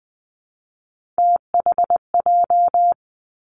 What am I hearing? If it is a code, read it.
Morse code, heard as a keyed tone, about 20 wpm, 700 Hz. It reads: THJ
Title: Three Challenging Letter-Number Combinations 20wpm